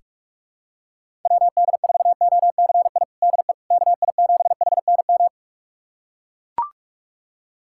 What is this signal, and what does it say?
Morse code, heard as a keyed tone, about 43 wpm, 700 Hz. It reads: WD4OCI DE KI7HNM
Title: USA Callsign DE Callsign - 6 Characters - 45wpm